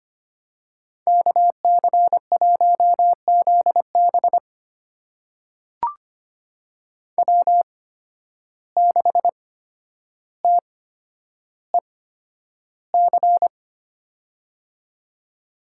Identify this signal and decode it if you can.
Morse code, heard as a keyed tone, about 25 wpm, spaced as for 11 wpm, 700 Hz. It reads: KC1ZB W6TEC
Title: USA Call Signs - 5 Characters - 8x Character Spacing - 25wpm